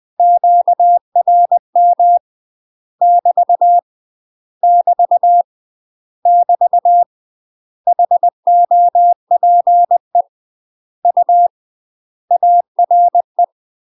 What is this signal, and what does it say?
Morse code, heard as a keyed tone, about 20 wpm, 700 Hz. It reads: QRM <BT> <BT> <BT> HOPE U ARE
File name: UK MASTER QSOs TRAINING / 20WPM / UK MASTER QSOs 0001